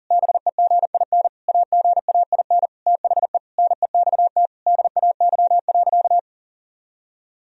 Morse code, heard as a keyed tone, about 40 wpm, 700 Hz. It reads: BEGIN AGAIN THE NEXT DAY.
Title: Sentences from Top 1000 Words - Review Part 1 - 40wpm